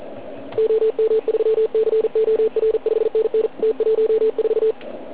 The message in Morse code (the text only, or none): OM3COR 5NN T14